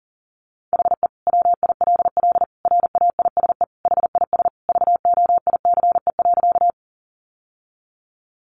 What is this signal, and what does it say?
Morse code, heard as a keyed tone, about 40 wpm, 700 Hz. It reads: HE WILL RAISE HIS VOICE.
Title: Sentences from Top 1400 Words 40wpm